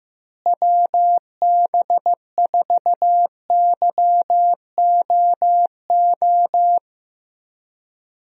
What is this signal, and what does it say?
Morse code, heard as a keyed tone, about 15 wpm, 700 Hz. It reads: WB4YOO
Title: USA Call Signs - 6 Characters - Repeated 3x - 15wpm